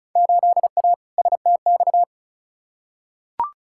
Morse code, heard as a keyed tone, about 36 wpm, 700 Hz. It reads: 8A STX
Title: Field Day Exchanges Encore - 35wpm